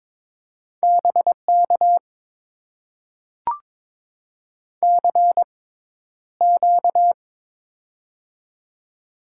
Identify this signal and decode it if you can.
Morse code, heard as a keyed tone, about 21 wpm, spaced as for 11 wpm, 700 Hz. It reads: BK CQ
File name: Top 90 QSO Elements - 6x Character Spacing - 22wpm